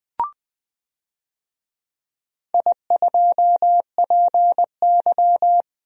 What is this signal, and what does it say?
Morse code, heard as a keyed tone, about 21 wpm, 700 Hz. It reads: I2PY
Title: DXCC-100 Call Signs - 4 Characters - Repeated 3x - 20wpm